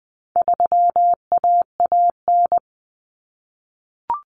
Morse code, heard as a keyed tone, about 20 wpm, 700 Hz. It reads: 3AAN
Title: Common Cut Numbers - 4-Digit 20wpm